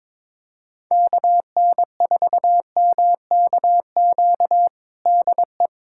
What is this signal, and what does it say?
Morse code, heard as a keyed tone, about 22 wpm, 700 Hz. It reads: KN4MKQ DE
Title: USA Callsign DE Callsign - 6 Characters - 22wpm